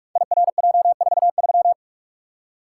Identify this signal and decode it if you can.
Morse code, heard as a keyed tone, about 44 wpm, 700 Hz. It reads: IRJ43
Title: Code Groups - 5 Letter:Numbers - 45wpm